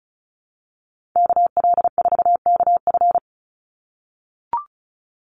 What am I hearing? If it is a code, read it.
Morse code, heard as a keyed tone, about 34 wpm, 700 Hz. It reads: KL4KF
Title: USA Call Signs 35wpm